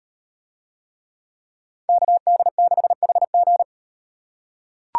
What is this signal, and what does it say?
Morse code, heard as a keyed tone, about 36 wpm, 700 Hz. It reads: KD6HG